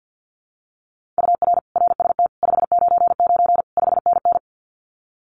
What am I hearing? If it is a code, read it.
Morse code, heard as a keyed tone, about 50 wpm, 700 Hz. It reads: UR RST 599 5NN
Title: Common QSO Phrases Encore 50wpm